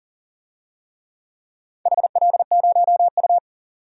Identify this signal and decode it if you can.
Morse code, heard as a keyed tone, about 40 wpm, 700 Hz. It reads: HL0U
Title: DXCC-100 Call Signs - 4 Characters - 40wpm